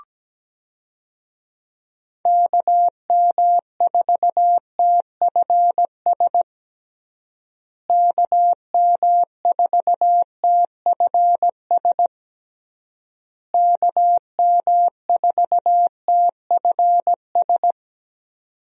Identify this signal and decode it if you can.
Morse code, heard as a keyed tone, about 17 wpm, 700 Hz. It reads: KM4TFS KM4TFS KM4TFS
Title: USA Call Signs - 6 Characters - Repeated 3x - 17wpm